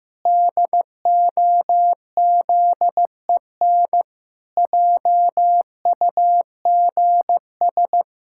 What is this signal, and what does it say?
Morse code, heard as a keyed tone, about 16 wpm, 700 Hz. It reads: DOZEN JUGS
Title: Pangram Challenge v2 - 15wpm